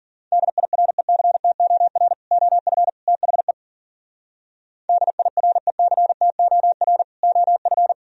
Morse code, heard as a keyed tone, about 52 wpm, 700 Hz. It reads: DIRECTOR OF THE DIRECTOR OF